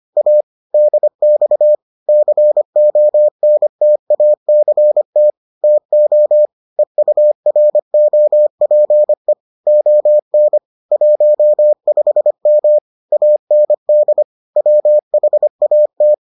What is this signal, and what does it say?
Morse code, heard as a keyed tone, about 25 wpm, 600 Hz. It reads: A DX CONTACT TO EUROPE ON 15M AND WHAT